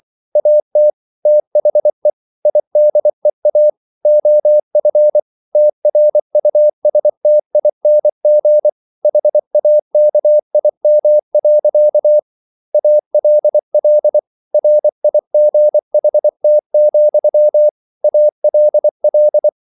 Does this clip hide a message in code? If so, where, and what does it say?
Yes: Morse code from 0.4 to 19.5 s, heard as a keyed tone, about 24 wpm, 600 Hz. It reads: AT THE IDEA OF TRUSTING HAKIM. ALL RIGHT, ALL